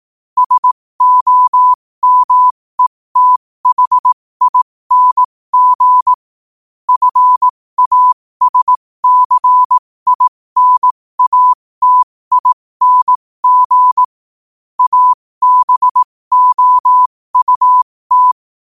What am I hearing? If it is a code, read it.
Morse code, heard as a keyed tone, about 18 wpm, spaced as for 15 wpm, 1000 Hz. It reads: SOMETHING FASCINATING ABOUT
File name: 15wpm